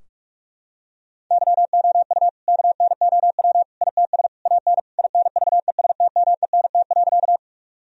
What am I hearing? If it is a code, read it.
Morse code, heard as a keyed tone, about 43 wpm, 700 Hz. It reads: YOU KNOW ITS AN INVESTMENT.
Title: Sentences from Top 1000 Words - Review Part 4 - 45wpm